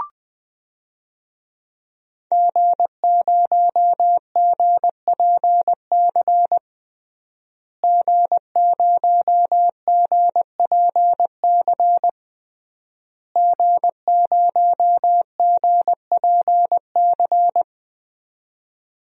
Morse code, heard as a keyed tone, about 21 wpm, 700 Hz. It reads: G0GPC G0GPC G0GPC
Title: DXCC-100 Call Signs - 5 Characters - Repeated 3x - 20wpm